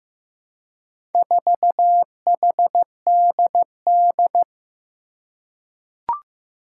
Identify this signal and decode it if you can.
Morse code, heard as a keyed tone, about 15 wpm, 700 Hz. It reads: 4HDD
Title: Four Challenging Letter-Number Combinations 15wpm